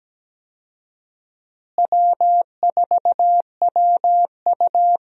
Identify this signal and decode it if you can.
Morse code, heard as a keyed tone, about 17 wpm, 700 Hz. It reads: W4WU
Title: USA Call Signs - 4 Characters - Repeated 2x - 17wpm